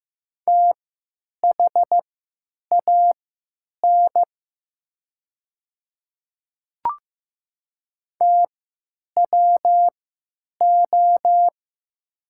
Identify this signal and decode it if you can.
Morse code, heard as a keyed tone, about 15 wpm, spaced as for 9 wpm, 700 Hz. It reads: THAN TWO
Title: 3x Character Spacing - Top 500 Words - 15wpm